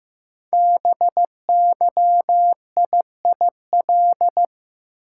Morse code, heard as a keyed tone, about 15 wpm, 700 Hz. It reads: BYIIL
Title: Code Groups - 5 Letters - 15wpm